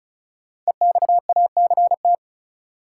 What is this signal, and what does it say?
Morse code, heard as a keyed tone, about 34 wpm, 700 Hz. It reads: EXACT
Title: Sentences from Top 1300 Words - New Word Drill Alt - 35wpm